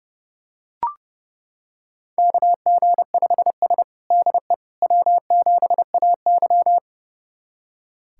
Morse code, heard as a keyed tone, about 32 wpm, 700 Hz. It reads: KG5S DE W7AY
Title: USA Callsign DE Callsign - 4 Characters - 30wpm